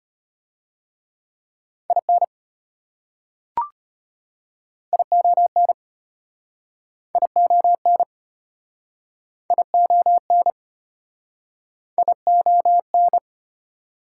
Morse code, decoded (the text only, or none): IN ION ION ION ION